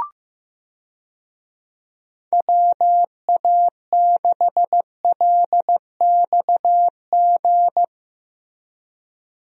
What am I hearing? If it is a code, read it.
Morse code, heard as a keyed tone, about 16 wpm, 700 Hz. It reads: WA6LXG
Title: USA Call Signs - 6 Characters - Repeated 3x - 15wpm